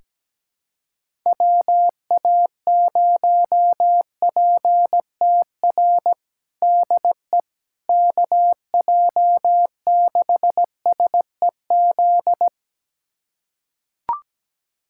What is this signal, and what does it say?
Morse code, heard as a keyed tone, about 17 wpm, 700 Hz. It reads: WA0PTR DE KJ6SEZ
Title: USA Callsign DE Callsign - 6 Characters - 17wpm